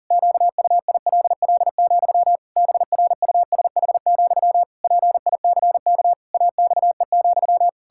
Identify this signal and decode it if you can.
Morse code, heard as a keyed tone, about 40 wpm, 700 Hz. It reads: QUILL, BRUSH, PICK AXE,
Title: Pangram Challenge v2 - 40wpm